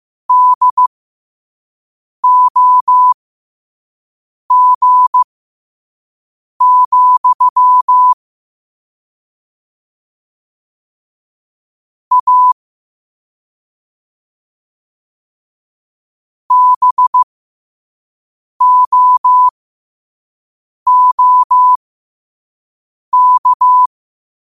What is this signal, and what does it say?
Morse code, heard as a keyed tone, about 15 wpm, spaced as for 5 wpm, 1000 Hz. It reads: DOG, A BOOK
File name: qotdfiles / 5wpm